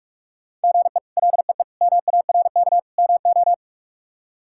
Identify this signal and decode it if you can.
Morse code, heard as a keyed tone, about 46 wpm, 700 Hz. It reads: GE LEE MARK MO
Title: K1USN SST QSO Practice - 45wpm